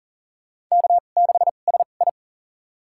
Morse code, heard as a keyed tone, about 40 wpm, spaced as for 29 wpm, 700 Hz. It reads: K6SI
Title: USA Call Signs - 2x Character Spacing - 40wpm